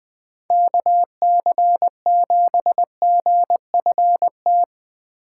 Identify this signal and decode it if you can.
Morse code, heard as a keyed tone, about 20 wpm, 700 Hz. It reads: KC7GFT